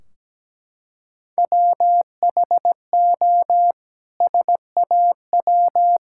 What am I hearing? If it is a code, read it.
Morse code, heard as a keyed tone, about 17 wpm, 700 Hz. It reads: WHO SAW
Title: Sentences from Top 1600 Words 17wpm